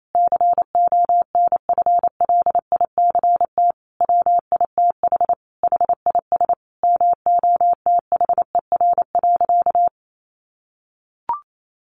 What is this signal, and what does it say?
Morse code, heard as a keyed tone, about 28 wpm, 700 Hz. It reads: CONFLICT WITH HIS MOTHER.
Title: Sentences from Top 1300 Words 28wpm